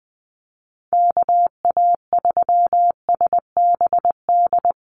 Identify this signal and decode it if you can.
Morse code, heard as a keyed tone, about 19 wpm, 700 Hz. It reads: KA3SBD
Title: USA Call Signs 20wpm